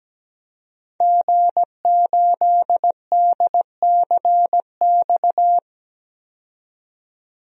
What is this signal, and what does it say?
Morse code, heard as a keyed tone, about 17 wpm, 700 Hz. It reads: G8DCX
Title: DXCC-100 Call Signs - 5 Characters - 17wpm